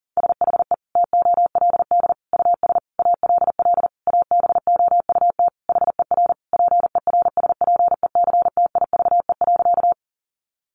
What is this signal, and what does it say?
Morse code, heard as a keyed tone, about 40 wpm, 700 Hz. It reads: SHE TOLD US ALL ABOUT HER PERSPECTIVE.
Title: Sentences from Top 1200 Words 40wpm